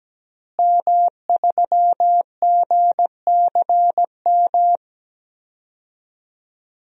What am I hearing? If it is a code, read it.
Morse code, heard as a keyed tone, about 17 wpm, 700 Hz. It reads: M3GCM
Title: DXCC-100 Call Signs - 5 Characters - Repeated 2x - 17wpm